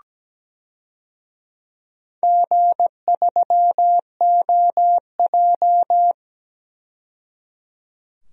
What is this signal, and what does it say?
Morse code, heard as a keyed tone, about 17 wpm, 700 Hz. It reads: G3OJ
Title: DXCC-100 Call Signs - 4 Characters - 17wpm